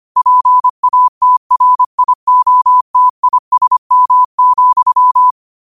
Morse code, heard as a keyed tone, about 25 wpm, 1000 Hz. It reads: PATRIOTISM,